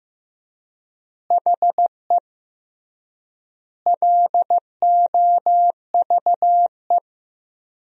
Morse code, heard as a keyed tone, about 14 wpm, 700 Hz. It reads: HE LOVE
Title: 3x Word-Spacing - Sentences from Top 400 Words - 15wpm